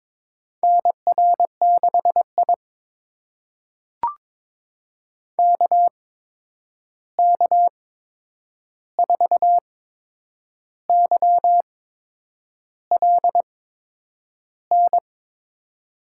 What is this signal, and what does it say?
Morse code, heard as a keyed tone, about 21 wpm, spaced as for 8 wpm, 700 Hz. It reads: NR6I KK4YLN
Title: USA Call Signs - 8x Character Spacing - 22wpm